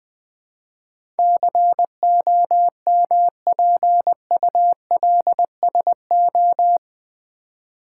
Morse code, heard as a keyed tone, about 20 wpm, 700 Hz. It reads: COMPULSO
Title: Build A Word - 4 Letters At A Time - 10-Letter Words - 20wpm